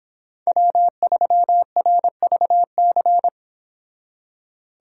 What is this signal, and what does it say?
Morse code, heard as a keyed tone, about 25 wpm, 700 Hz. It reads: W3RVC